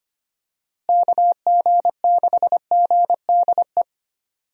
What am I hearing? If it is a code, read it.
Morse code, heard as a keyed tone, about 25 wpm, 700 Hz. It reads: KG6GDE
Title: USA Call Signs - 6 Characters - Repeated 2x - 25wpm